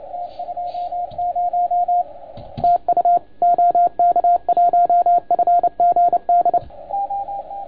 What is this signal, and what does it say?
Morse code, heard as a keyed tone, about 29 wpm, 700 Hz, with radio noise. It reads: T20 TU OK1FGD
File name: Archiv / CUC287 / OK1FGD CUC 287